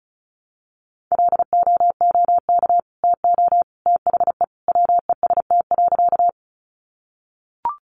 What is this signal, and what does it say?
Morse code, heard as a keyed tone, about 34 wpm, 700 Hz. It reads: LOOK TO THE WEST.
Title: Sentences from Top 400 Words 35wpm